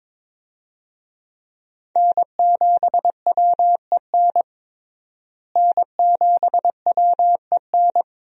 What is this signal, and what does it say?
Morse code, heard as a keyed tone, about 21 wpm, 700 Hz. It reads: N7WEN N7WEN
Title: USA Call Signs - 5 Characters - Repeated 3x - 22wpm